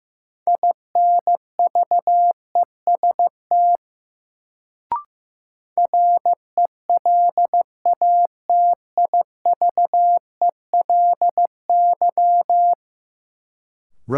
Morse code, heard as a keyed tone, about 15 wpm, 700 Hz. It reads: INVEST RELATIVELY
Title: Sentences from Top 1100 Words - New Word Drill - 15wpm